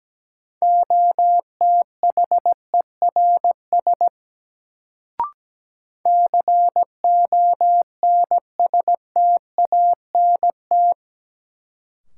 Morse code, heard as a keyed tone, about 17 wpm, 700 Hz. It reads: OTHERS CONSTANT
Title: Sentences from Top 1500 Words - New Word Drill Alt - 17wpm